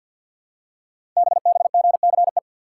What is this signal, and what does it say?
Morse code, heard as a keyed tone, about 48 wpm, 700 Hz. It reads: BBGCE
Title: Code Groups - 5 Letter:Numbers - 50wpm